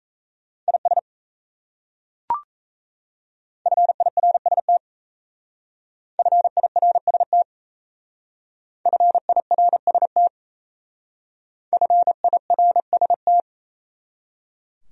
Morse code, heard as a keyed tone, about 35 wpm, 700 Hz. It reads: IS FIRST FIRST FIRST FIRST
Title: Speed-Racing Top 100 Words 28wpm